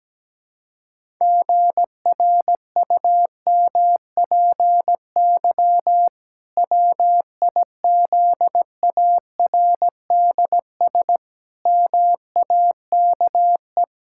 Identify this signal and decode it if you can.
Morse code, heard as a keyed tone, about 17 wpm, 700 Hz. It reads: GRUMPY WIZARDS MAKE
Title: Pangram Challenge v2 - 17wpm